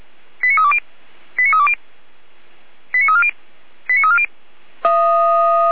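Five-tone selective call (ZVEI), heard as quick runs of five tones, tones of about 70 ms. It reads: 89320 89320 89340 89340
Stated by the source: Received on an SDR as a ZVEI five-tone call, address 89315.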